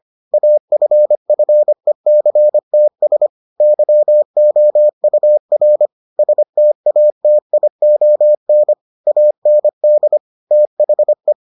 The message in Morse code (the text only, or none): AFFECTS YOUR STATION AND THE